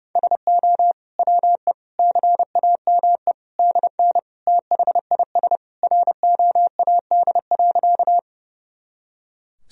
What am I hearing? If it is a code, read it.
Morse code, heard as a keyed tone, about 30 wpm, 700 Hz. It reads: SO WE CAME DN THIS ROAD.
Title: Sentences from Top 400 Words and Abbreviations 30wpm